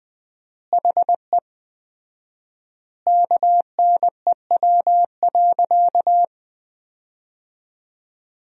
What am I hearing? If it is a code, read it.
Morse code, heard as a keyed tone, about 20 wpm, 700 Hz. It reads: HE KNEW.
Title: 4x Word-Spacing - Sentences from Top 400 Words - 20wpm